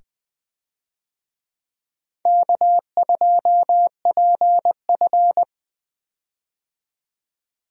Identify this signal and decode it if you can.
Morse code, heard as a keyed tone, about 20 wpm, 700 Hz. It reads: K2PF
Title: USA Call Signs - 4 Characters - 20wpm